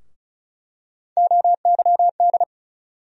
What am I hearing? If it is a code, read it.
Morse code, heard as a keyed tone, about 36 wpm, 700 Hz. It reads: OYD
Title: USA Call Sign Postfixes - 3 Letters 35wpm